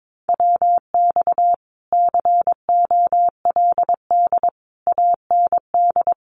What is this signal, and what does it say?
Morse code, heard as a keyed tone, about 22 wpm, 700 Hz. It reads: WX COLD AND
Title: Common QSO Phrases Encore 22wpm